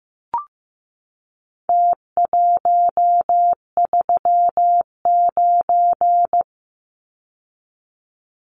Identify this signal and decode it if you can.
Morse code, heard as a keyed tone, about 15 wpm, 700 Hz. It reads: T139
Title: Numbers - Contest Style - 4-Digit 15wpm